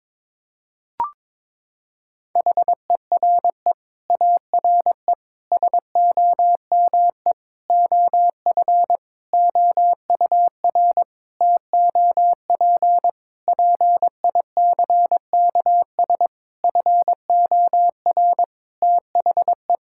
Morse code, heard as a keyed tone, about 22 wpm, 700 Hz. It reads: HERE ARE SOME OF OUR TOP PICKS FOR THE
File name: Sentences with 9-Word Phrases - 22wpm